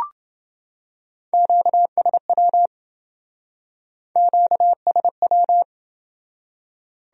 Morse code, heard as a keyed tone, about 27 wpm, 700 Hz. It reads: QSW QSW